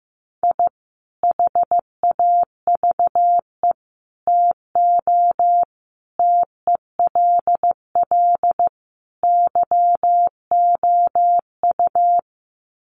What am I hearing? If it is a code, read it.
Morse code, heard as a keyed tone, about 15 wpm, 700 Hz. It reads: I HAVE TO TELL YOU
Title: Sets of 5 Words 15wpm